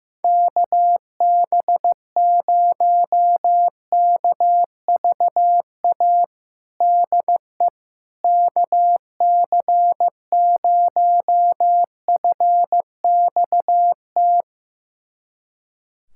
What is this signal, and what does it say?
Morse code, heard as a keyed tone, about 16 wpm, 700 Hz. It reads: KB0KVA DE KC0FXT
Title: USA Callsign DE Callsign - 6 Characters - 15wpm